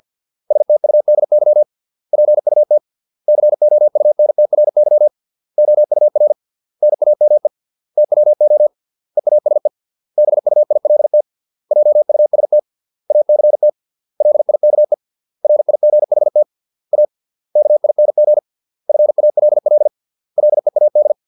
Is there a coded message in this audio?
Yes: Morse code from 0.5 to 21.1 s, heard as a keyed tone, about 50 wpm, 600 Hz. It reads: STUDY PUT COUNTRY OUR NAME TWO EASE BUILT JUST ACT LIKE RIGHT A KING FALL LEAD